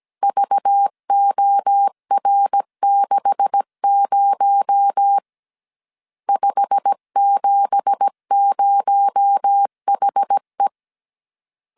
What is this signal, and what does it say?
Morse code, heard as a keyed tone, about 17 wpm, 800 Hz. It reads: VOR60 570HE
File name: CW / DET / 1 9 1500C S A 16wpm 2828